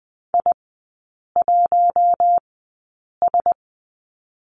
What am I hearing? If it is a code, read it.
Morse code, heard as a keyed tone, about 20 wpm, spaced as for 8 wpm, 700 Hz. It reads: I1S
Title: 053 Lesson 10e - Letter r - Three Characters 20wpm